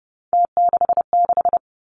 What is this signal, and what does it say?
Morse code, heard as a keyed tone, about 30 wpm, 700 Hz. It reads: T66
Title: Three Challenging Letter-Number Combinations 30wpm